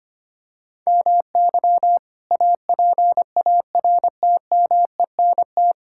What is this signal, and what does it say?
Morse code, heard as a keyed tone, about 25 wpm, 700 Hz. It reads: MY APARTMENT
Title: Sentences from Top 1000 Words - Review Part 6 - 25wpm